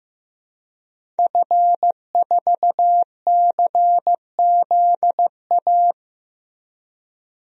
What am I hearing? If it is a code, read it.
Morse code, heard as a keyed tone, about 15 wpm, 700 Hz. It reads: F4CZA
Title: DXCC-100 Call Signs - 5 Characters - 15wpm